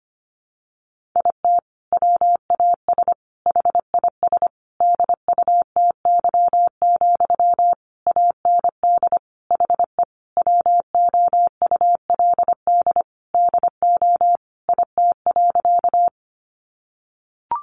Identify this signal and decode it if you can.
Morse code, heard as a keyed tone, about 25 wpm, 700 Hz. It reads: IT WAS HIS DUTY, AND HE WOULD DO IT.
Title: Sentences from Top 1500 Words 25wpm